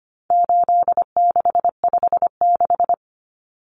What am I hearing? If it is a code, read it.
Morse code, heard as a keyed tone, about 25 wpm, 700 Hz. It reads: 8656
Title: Common Cut Numbers - 4-Digit 25wpm